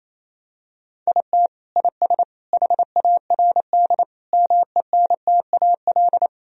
Morse code, heard as a keyed tone, about 28 wpm, 700 Hz. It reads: IT IS HARD MENTAL
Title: Sentences from Top 1000 Words - Review Part 4 - 28wpm